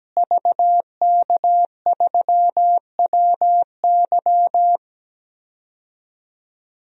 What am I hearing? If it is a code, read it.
Morse code, heard as a keyed tone, about 17 wpm, 700 Hz. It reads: VK3WY
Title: DXCC-100 Call Signs - 5 Characters - Repeated 2x - 17wpm